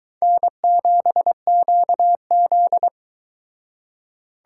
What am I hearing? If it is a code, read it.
Morse code, heard as a keyed tone, about 22 wpm, 700 Hz. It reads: N7QZ